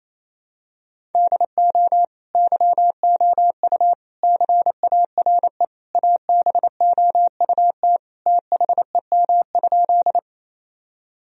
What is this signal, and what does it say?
Morse code, heard as a keyed tone, about 27 wpm, 700 Hz. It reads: DO YOU CARE ABOUT THEM?
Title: Sentences from Top 1000 Words - Review Part 1 - 28wpm